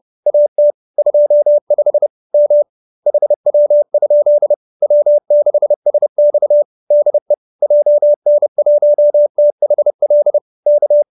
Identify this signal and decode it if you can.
Morse code, heard as a keyed tone, about 30 wpm, 570 Hz. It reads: AT 25 M HW? W6SX DE JN1THL K